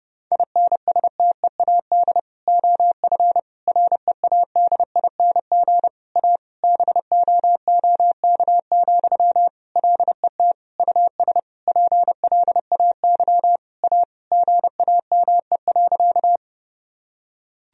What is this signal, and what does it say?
Morse code, heard as a keyed tone, about 32 wpm, 700 Hz. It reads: INSTEAD OF READING A BOOK, LET US PLAY A GAME.
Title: Sentences from Top 1000 Words - Review Part 4 - 30wpm